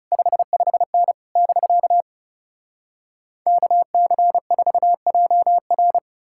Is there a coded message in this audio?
Yes: Morse code from 0.1 to 6.0 s, heard as a keyed tone, about 40 wpm, 700 Hz. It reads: 55N <BK> KC4JR